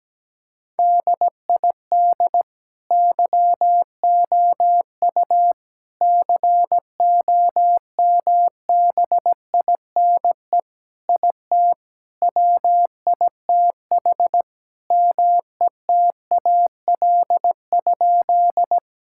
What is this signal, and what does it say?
Morse code, heard as a keyed tone, about 17 wpm, 700 Hz. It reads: DID YOU COMBINE IT WITH METAL?
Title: Sentences from Top 1600 Words 17wpm